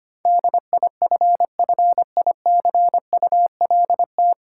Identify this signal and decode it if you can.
Morse code, heard as a keyed tone, about 25 wpm, 700 Hz. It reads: DIFFICULT
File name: practice / 25wpm / 9 Letter Words - First 3 Letters Given - 25wpm